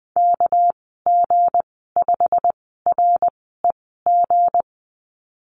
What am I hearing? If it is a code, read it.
Morse code, heard as a keyed tone, about 19 wpm, spaced as for 14 wpm, 700 Hz. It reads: KG5REG
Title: 200 Lesson 32g - Letter k - Call Signs 20wpm